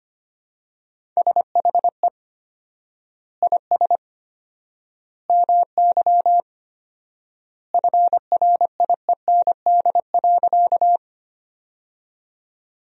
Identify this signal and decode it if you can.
Morse code, heard as a keyed tone, about 25 wpm, 700 Hz. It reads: SHE IS MY FRIEND.
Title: 4x Word-Spacing - Sentences from Top 300 Words - 25wpm